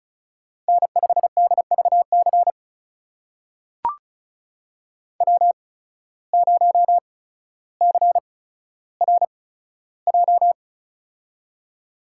Morse code, heard as a keyed tone, about 36 wpm, spaced as for 15 wpm, 690 Hz. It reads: N5DVC W0CRJ
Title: USA Call Signs - 5 Characters - 8x Character Spacing - 35wpm